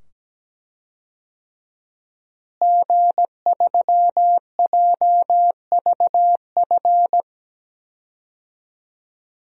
Morse code, heard as a keyed tone, about 17 wpm, 700 Hz. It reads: G3JVF
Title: DXCC-100 Call Signs - 5 Characters - Repeated 2x - 17wpm